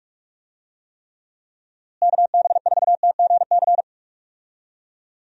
Morse code, heard as a keyed tone, about 41 wpm, 700 Hz. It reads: KB4TGC
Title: USA Call Signs - 6 Characters - Repeated 2x - 45wpm